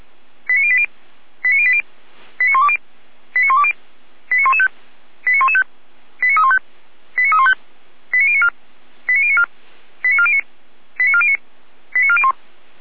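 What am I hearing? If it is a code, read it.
Five-tone selective call (ZVEI), heard as quick runs of five tones, tones of about 70 ms. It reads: 89080 89080 89120 89120 89115 89115 89315 89315 89094 89094 89409 89409 89401
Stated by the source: ZVEI five-tone selective call received by SDR.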